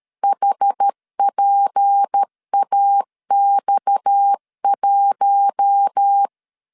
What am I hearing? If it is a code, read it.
Morse code, heard as a keyed tone, about 13 wpm, 800 Hz. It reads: HPAX1
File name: LEZIONI / 8 YJ45 / 8 YJ4 12wpm 2151